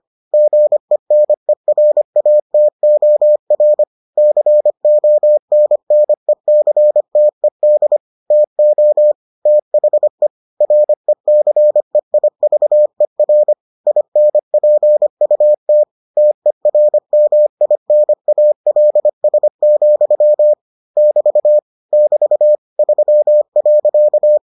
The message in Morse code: GENERATOR CONNECTED TO THE RECEIVER INPUT TERMINALS, <BT> <BT> 3.

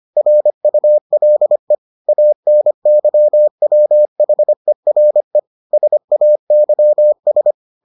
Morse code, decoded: RULE ANYWHERE SAYS

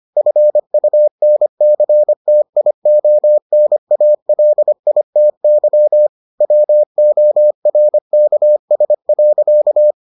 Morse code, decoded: FUNCTIONALITY WORKS.

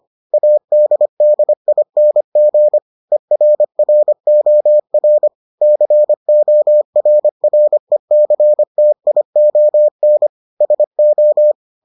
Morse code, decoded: ADDING ERROR CORRECTION SO

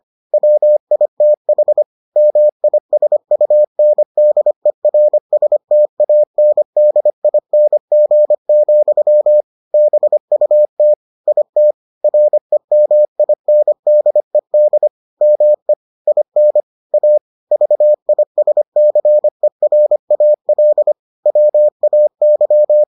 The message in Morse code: WITH MISUNDERSTANDING, BUT IT REMINDED ME IN A VISCERAL WAY

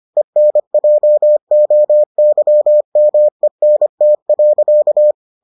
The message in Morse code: ENJOYMENT.

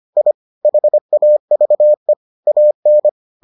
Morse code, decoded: I HAVE AN

